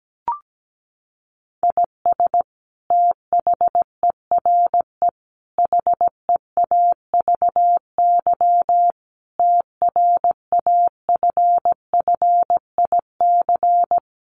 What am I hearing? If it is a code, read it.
Morse code, heard as a keyed tone, about 17 wpm, 700 Hz. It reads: IS THERE HEAVY TRAFFIC
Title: Sentences from Top 1300 Words 17wpm